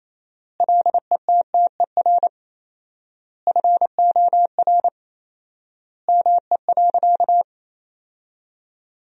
Morse code, heard as a keyed tone, about 27 wpm, 700 Hz. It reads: LETTER FOR ME.
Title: 4x Word-Spacing - Sentences from Top 300 Words - 28wpm